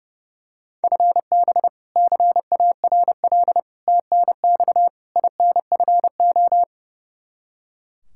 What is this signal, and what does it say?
Morse code, heard as a keyed tone, about 30 wpm, 700 Hz. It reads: FB CARL TNX INFO
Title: Practice QSO - Exchange 3 - More Info - 30wpm